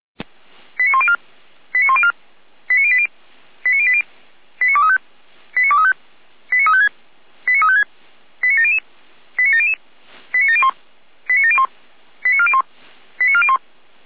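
Five-tone selective call (ZVEI), heard as quick runs of five tones, tones of about 70 ms. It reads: 89104 89104 89080 89080 89235 89235 89356 89356 89700 89700 89701 89701 89401 89401
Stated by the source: ZVEI five-tone selective call received on SDR.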